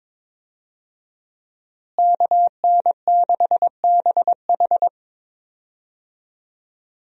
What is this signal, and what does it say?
Morse code, heard as a keyed tone, about 21 wpm, 700 Hz. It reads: KN6BH
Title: USA Call Signs - 5 Characters - Repeated 2x - 22wpm